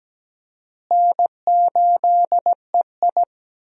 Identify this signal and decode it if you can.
Morse code, heard as a keyed tone, about 17 wpm, 700 Hz. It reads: N8EI
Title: USA Call Signs - 4 Characters - Repeated 2x - 17wpm